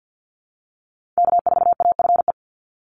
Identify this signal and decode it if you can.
Morse code, heard as a keyed tone, about 50 wpm, 700 Hz. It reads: K4AFE
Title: Call Signs Repeated 2x - 50wpm